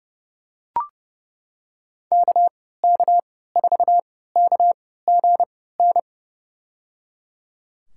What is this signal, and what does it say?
Morse code, heard as a keyed tone, about 31 wpm, spaced as for 17 wpm, 700 Hz. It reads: KK4KGN
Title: USA Call Signs - 6 Characters - 3x Character Spacing - 30wpm